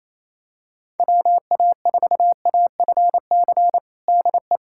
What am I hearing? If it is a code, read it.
Morse code, heard as a keyed tone, about 28 wpm, 700 Hz. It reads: WA4AFC DE
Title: USA Callsign DE Callsign - 6 Characters - 28wpm